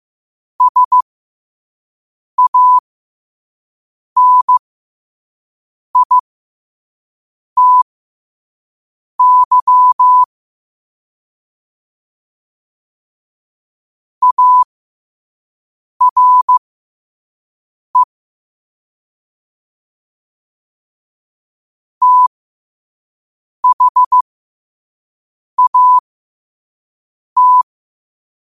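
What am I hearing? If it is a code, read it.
Morse code, heard as a keyed tone, about 15 wpm, spaced as for 5 wpm, 1000 Hz. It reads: SANITY ARE THAT
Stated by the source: Morse code at 5 wpm.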